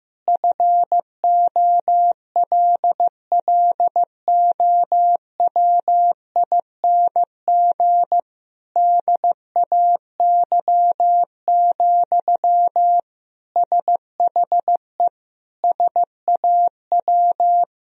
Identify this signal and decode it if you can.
Morse code, heard as a keyed tone, about 15 wpm, 700 Hz. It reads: FOLLOWING DAY, SHE SAW
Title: Sentences from Top 1000 Words - Review Part 4 - 15wpm